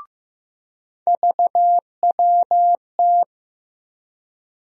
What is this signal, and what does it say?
Morse code, heard as a keyed tone, about 15 wpm, 700 Hz. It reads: VWT
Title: USA Call Sign Postfixes - 3 Letters 15wpm